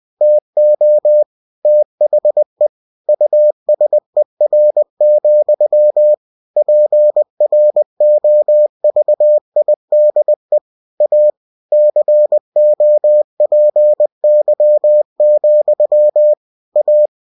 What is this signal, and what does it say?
Morse code, heard as a keyed tone, about 20 wpm, 600 Hz. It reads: TO THE USER, PROVIDE A COPY, A
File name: cw-books / winnie-the-pooh / winnie-20wpm-page-0116